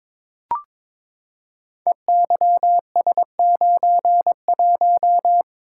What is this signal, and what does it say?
Morse code, heard as a keyed tone, about 22 wpm, 700 Hz. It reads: EYS91
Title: Code Groups - 5 Letter:Numbers - 22wpm